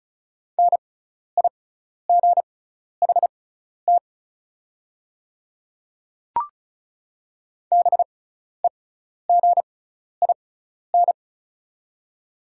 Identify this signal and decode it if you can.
Morse code, heard as a keyed tone, about 36 wpm, spaced as for 12 wpm, 690 Hz. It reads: NIGHT BEGIN
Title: 6x Character Spacing - Top 500 Words - 35wpm